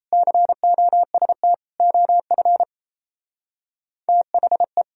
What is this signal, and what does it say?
Morse code, heard as a keyed tone, about 33 wpm, 700 Hz. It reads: COST OF THE
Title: Speed-Racing - Sets of 3 Words 28wpm